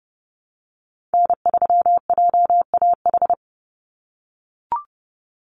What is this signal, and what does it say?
Morse code, heard as a keyed tone, about 30 wpm, 700 Hz. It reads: N3JAH
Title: USA Call Signs 30wpm